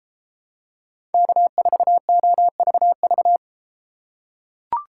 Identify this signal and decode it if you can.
Morse code, heard as a keyed tone, about 33 wpm, 700 Hz. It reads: K4OVV